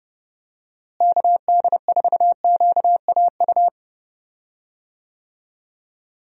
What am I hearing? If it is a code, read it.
Morse code, heard as a keyed tone, about 30 wpm, 700 Hz. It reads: KD4QAU
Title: USA Call Signs - 6 Characters - Repeated 3x - 30wpm